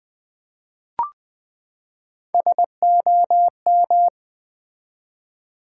Morse code, heard as a keyed tone, about 20 wpm, 700 Hz. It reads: SOM
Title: Build A Word - 3 Letters At A Time - 9-Letter Words - 20wpm